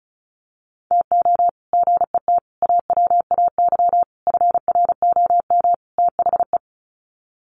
Morse code, heard as a keyed tone, about 34 wpm, 700 Hz. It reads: TO GET AWAY FROM THE
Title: Sets of 5 Words Encore2 35wpm